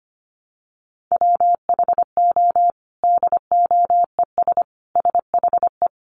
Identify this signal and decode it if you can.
Morse code, heard as a keyed tone, about 25 wpm, 700 Hz. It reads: WHO DOES SHE
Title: Sentences from Top 1300 Words 25wpm